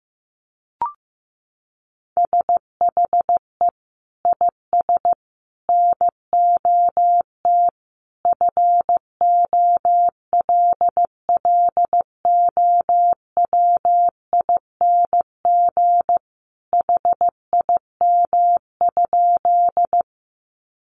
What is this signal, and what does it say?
Morse code, heard as a keyed tone, about 15 wpm, 700 Hz. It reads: SHE IS NOT FOLLOWING HIM?
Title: Sentences from Top 700 Words 15wpm